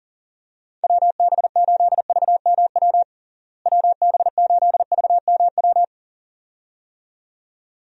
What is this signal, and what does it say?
Morse code, heard as a keyed tone, about 40 wpm, 700 Hz. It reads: WB8VMW WB8VMW
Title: USA Call Signs - 6 Characters - Repeated 2x - 40wpm